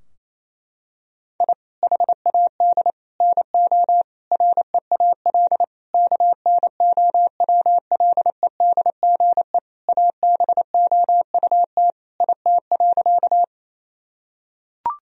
Morse code, heard as a keyed tone, about 28 wpm, 700 Hz. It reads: I HAD NO REAL KNOWLEDGE ABOUT IT.
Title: Sentences from Top 1000 Words - Review Part 3 - 28wpm